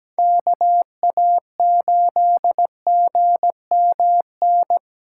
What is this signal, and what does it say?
Morse code, heard as a keyed tone, about 17 wpm, 700 Hz. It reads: KA8GMN
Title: USA Callsign DE Callsign 17wpm